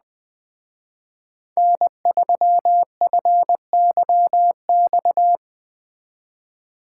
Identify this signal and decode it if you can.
Morse code, heard as a keyed tone, about 20 wpm, 700 Hz. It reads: N3FYX